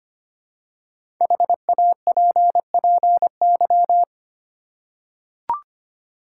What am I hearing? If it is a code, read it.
Morse code, heard as a keyed tone, about 25 wpm, 700 Hz. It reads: HAPPY
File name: 5 Letter Words - First 3 Letters Given - 25wpm